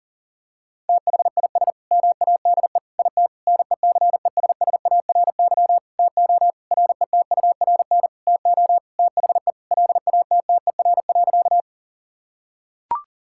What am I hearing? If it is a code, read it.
Morse code, heard as a keyed tone, about 40 wpm, 700 Hz. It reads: THIS MADE IT NECESSARY TO RETURN TO THE LATTER.
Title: Sentences from Top 1600 Words 40wpm